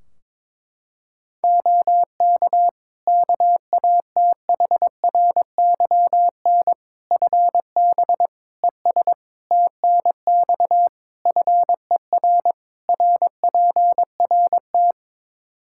Morse code, heard as a keyed tone, about 21 wpm, 700 Hz. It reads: OK KATHRYN FB ES TNX FER RPRT
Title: Practice QSO - Exchange 2 - Info - 22wpm